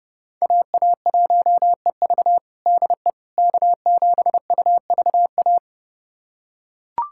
Morse code, heard as a keyed tone, about 30 wpm, 700 Hz. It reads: AA1EV DE K7UVA
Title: USA Callsign DE Callsign - 5 Characters - 30wpm